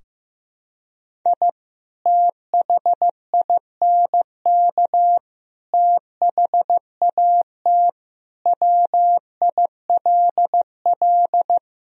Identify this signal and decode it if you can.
Morse code, heard as a keyed tone, about 15 wpm, 700 Hz. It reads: I THINK THAT WILL
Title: Sentences from Top 1000 Words - Review Part 2 - 15wpm